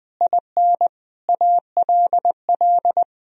Morse code, heard as a keyed tone, about 21 wpm, 700 Hz. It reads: IN ALL
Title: Sentences from Top 1000 Words - Review Part 3 - 20wpm